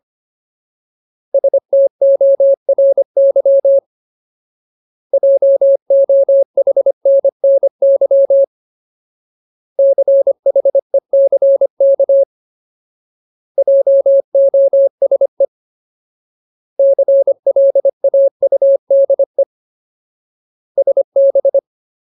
Morse code, heard as a keyed tone, about 25 wpm, 550 Hz. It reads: STORY JOHNNY CHECK JOSE CLAUDE SB
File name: Ultimate Word List / 25WPM Ultimate